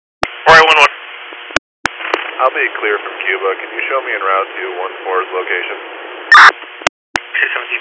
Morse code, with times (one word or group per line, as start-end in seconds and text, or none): none